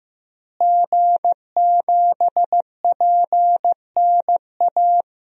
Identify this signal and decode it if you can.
Morse code, heard as a keyed tone, about 15 wpm, 700 Hz. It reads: G7PNA
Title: DXCC-100 Call Signs - 5 Characters - Repeated 3x - 15wpm